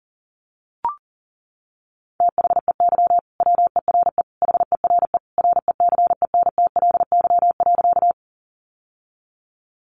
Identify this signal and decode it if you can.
Morse code, heard as a keyed tone, about 37 wpm, 700 Hz. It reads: THEY WERE HERE RECENTLY.
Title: Sentences from Top 1200 Words 40wpm